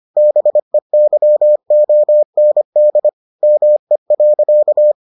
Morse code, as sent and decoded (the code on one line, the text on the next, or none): -... -.-- --- -. -.. / -- . .-.-.-
BEYOND ME.